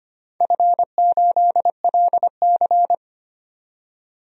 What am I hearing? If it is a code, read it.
Morse code, heard as a keyed tone, about 25 wpm, 700 Hz. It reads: F8LC
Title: DXCC-100 Call Signs - 4 Characters - 25wpm